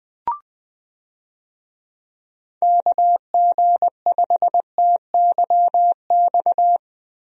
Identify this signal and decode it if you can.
Morse code, heard as a keyed tone, about 20 wpm, 700 Hz. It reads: KG5TYX
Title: USA Call Signs - 6 Characters - Repeated 3x - 20wpm